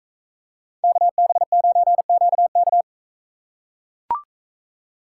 Morse code, heard as a keyed tone, about 42 wpm, 700 Hz. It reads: KB9QK